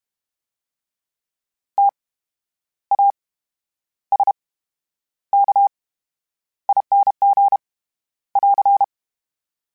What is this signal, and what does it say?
Morse code, heard as a keyed tone, about 32 wpm, spaced as for 10 wpm, 800 Hz. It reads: TASKING <AR>
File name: Special INGs / Special INGs-mp3s / VerySlowToFaster Ing @30